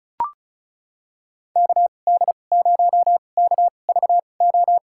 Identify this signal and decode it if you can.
Morse code, heard as a keyed tone, about 34 wpm, spaced as for 25 wpm, 700 Hz. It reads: KD0KVO
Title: USA Call Signs - 6 Characters - 2x Character Spacing - 35wpm